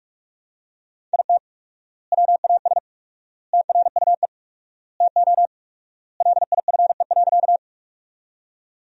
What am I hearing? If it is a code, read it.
Morse code, heard as a keyed tone, about 43 wpm, 700 Hz. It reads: IT WAS TRUE TO LIFE.
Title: 4x Word-Spacing - Sentences from Top 400 Words - 45wpm